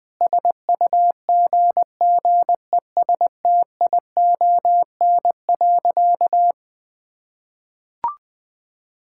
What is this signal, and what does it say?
Morse code, heard as a keyed tone, about 21 wpm, 700 Hz. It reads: SUGGESTION.
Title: Sentences from Top 1600 Words 20wpm